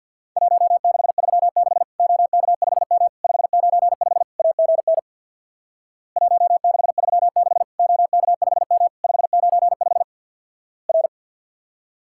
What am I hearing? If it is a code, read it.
Morse code, heard as a keyed tone, about 50 wpm, 700 Hz. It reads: 1636 OK5M 595 AGN 1636 OK5M 595 R
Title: Worked All Europe DX Contest - QTC Exchange with Repeats - 50wpm